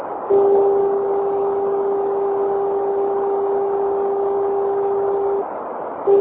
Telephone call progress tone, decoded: dial tone